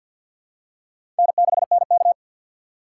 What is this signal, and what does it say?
Morse code, heard as a keyed tone, about 53 wpm, 700 Hz. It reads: N6NK
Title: USA Call Signs - 4 Characters - 6x Character Spacing - 50wpm